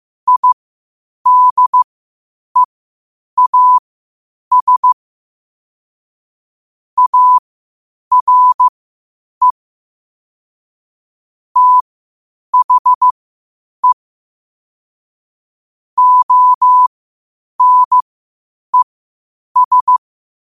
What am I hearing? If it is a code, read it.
Morse code, heard as a keyed tone, about 15 wpm, spaced as for 8 wpm, 1000 Hz. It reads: IDEAS ARE THE ONES
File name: qotdfiles / 8wpm